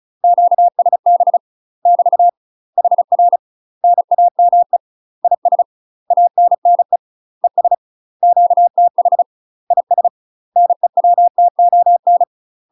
Morse code, heard as a keyed tone, about 34 wpm, 700 Hz. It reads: QSB <BT> HR NAME IS ANNE ES QTH IS NEWTON
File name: UK MASTER QSOs TRAINING / 35WPM / UK MASTER QSOs 0024